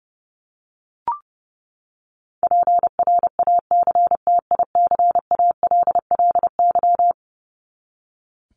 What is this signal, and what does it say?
Morse code, heard as a keyed tone, about 30 wpm, 700 Hz. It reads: PRACTICALLY
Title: LY Suffix - Top 100 Words with Base Words 30wpm